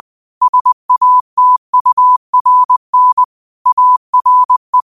Morse code, heard as a keyed tone, about 20 wpm, 1000 Hz. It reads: SATURN ARE